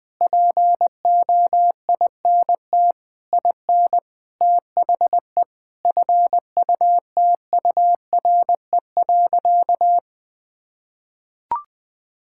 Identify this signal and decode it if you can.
Morse code, heard as a keyed tone, about 20 wpm, 700 Hz. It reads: POINT IN THE FUTURE.
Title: Sentences with 6-Word Phrases - Spoken Beginning - 20wpm